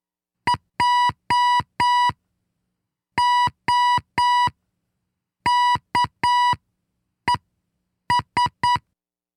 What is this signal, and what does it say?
Morse code, heard as a keyed tone, about 9 wpm, spaced as for 6 wpm, 980 Hz. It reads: JOKES
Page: This is the tenth transmission in our monthly Morse Code contest celebrating the centennial of Wisconsin Public Radio in 2017.